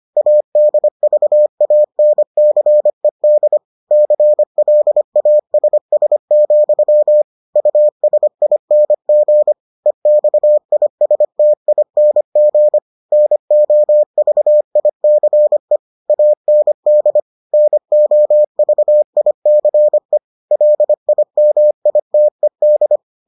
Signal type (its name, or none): Morse code